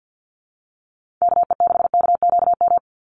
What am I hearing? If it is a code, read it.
Morse code, heard as a keyed tone, about 50 wpm, 700 Hz. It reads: KE6KQM
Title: USA Call Signs 50wpm